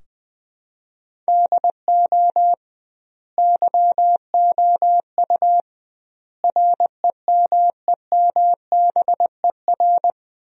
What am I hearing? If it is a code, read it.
Morse code, heard as a keyed tone, about 20 wpm, 700 Hz. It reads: DO YOU REMEMBER
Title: 2x Word-Spacing - Sentences from Top 400 Words - 20wpm